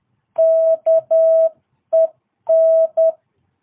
Morse code, heard as a keyed tone, about 10 wpm, 660 Hz. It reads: KEN
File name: jh8cbh / 2022cw